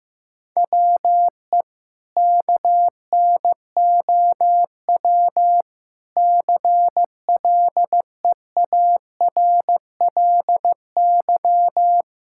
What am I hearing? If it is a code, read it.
Morse code, heard as a keyed tone, about 15 wpm, 700 Hz. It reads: WE KNOW CLEARLY
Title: Sentences from Top 1600 Words 15wpm